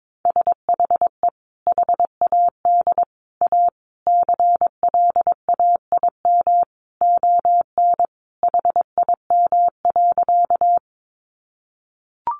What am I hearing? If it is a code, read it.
Morse code, heard as a keyed tone, about 22 wpm, 700 Hz. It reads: SHE HAD A CLAIM ON HIM.
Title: Sentences from Top 1500 Words 22wpm